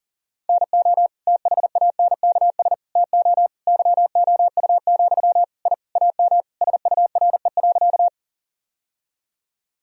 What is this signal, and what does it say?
Morse code, heard as a keyed tone, about 37 wpm, 700 Hz. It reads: NO THANKS TO YOU, I AM SURE.
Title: Sentences from Top 1000 Words - Review Part 3 - 40wpm